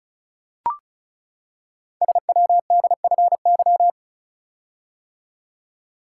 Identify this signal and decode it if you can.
Morse code, heard as a keyed tone, about 36 wpm, 700 Hz. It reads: SWDFY
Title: Code Groups - 5 Letters - 35wpm